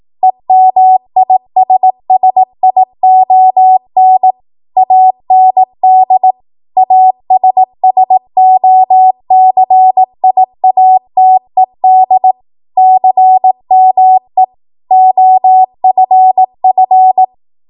Morse code, heard as a keyed tone, about 18 wpm, 750 Hz. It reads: EMISSION AND ASSOCIATED CME OFF